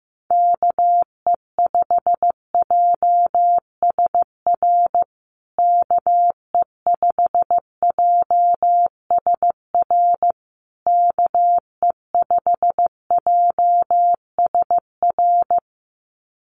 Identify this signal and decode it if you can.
Morse code, heard as a keyed tone, about 15 wpm, 700 Hz. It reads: KE5JSR KE5JSR KE5JSR
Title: USA Call Signs Repeated 3x 15wpm